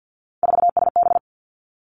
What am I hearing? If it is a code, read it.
Morse code, heard as a keyed tone, about 50 wpm, 700 Hz. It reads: 4SB
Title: Three Challenging Letter-Number Combinations 50wpm